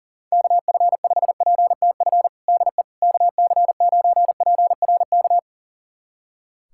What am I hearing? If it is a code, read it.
Morse code, heard as a keyed tone, about 40 wpm, 700 Hz. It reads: KF5PTF DE KC9PRK
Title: USA Callsign DE Callsign - 6 Characters - 40wpm